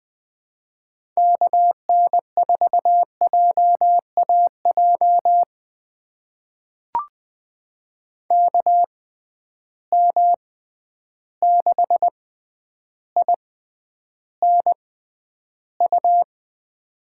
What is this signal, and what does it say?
Morse code, heard as a keyed tone, about 20 wpm, spaced as for 10 wpm, 700 Hz. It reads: KN4JAJ KM6INU
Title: USA Call Signs - 6x Character Spacing - 20wpm